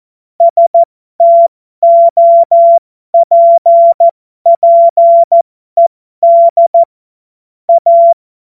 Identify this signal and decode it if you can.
Morse code, heard as a keyed tone, about 14 wpm, spaced as for 12 wpm, 680 Hz. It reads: STOPPED A
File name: Morsekurs / Witze